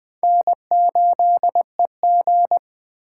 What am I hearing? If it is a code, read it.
Morse code, heard as a keyed tone, about 21 wpm, 700 Hz. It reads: N8EG
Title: Speed Racing - USA 1x2 Call Signs 17wpm